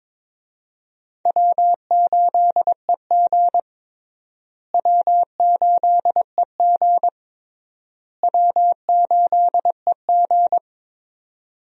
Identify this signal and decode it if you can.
Morse code, heard as a keyed tone, about 21 wpm, 700 Hz. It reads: W8EG W8EG W8EG
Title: USA Call Signs - 4 Characters - Repeated 3x - 22wpm